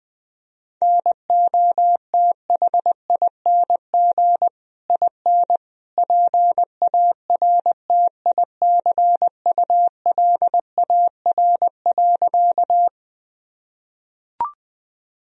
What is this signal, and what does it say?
Morse code, heard as a keyed tone, about 19 wpm, 700 Hz. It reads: NOTHING IN PARTICULAR.
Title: Sentences from Top 1000 Words - Review Part 4 - 20wpm